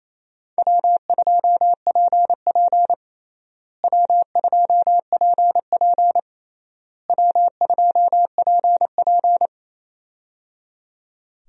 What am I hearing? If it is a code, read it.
Morse code, heard as a keyed tone, about 28 wpm, 700 Hz. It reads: W2PP W2PP W2PP
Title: USA Call Signs - 4 Characters - Repeated 3x - 28wpm